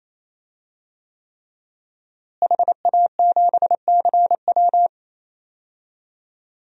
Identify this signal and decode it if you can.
Morse code, heard as a keyed tone, about 28 wpm, 700 Hz. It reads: HA7CW
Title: DXCC-100 Call Signs - 5 Characters - Repeated 3x - 28wpm